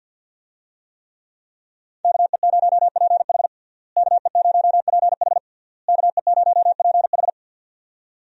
Morse code, heard as a keyed tone, about 51 wpm, 700 Hz. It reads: KE0PH KE0PH KE0PH
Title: USA Call Signs - 5 Characters - Repeated 3x - 50wpm